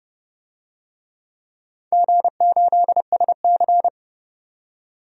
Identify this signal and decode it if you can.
Morse code, heard as a keyed tone, about 30 wpm, 700 Hz. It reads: G8SC
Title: DXCC-100 Call Signs - 4 Characters - 30wpm